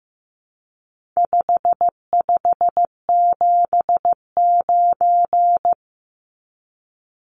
Morse code, heard as a keyed tone, about 15 wpm, 700 Hz. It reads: 5579
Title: Numbers - 4-Digit 15wpm